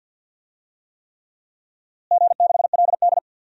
Morse code, heard as a keyed tone, about 50 wpm, 700 Hz. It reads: G6LD
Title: DXCC-100 Call Signs - 4 Characters - Repeated 3x - 50wpm